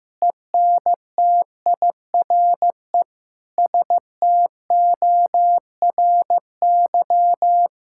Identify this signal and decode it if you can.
Morse code, heard as a keyed tone, about 16 wpm, 700 Hz. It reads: ENTIRE STORY
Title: Sentences from Top 1000 Words - Review Part 5 - 15wpm